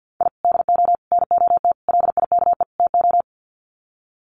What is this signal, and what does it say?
Morse code, heard as a keyed tone, about 50 wpm, 700 Hz. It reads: I DO NOT LIKE TO
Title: Sets of 5 Words 50wpm